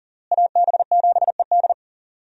Morse code, heard as a keyed tone, about 37 wpm, 700 Hz. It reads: AB7ED
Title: USA Call Signs - 5 Characters - Repeated 3x - 40wpm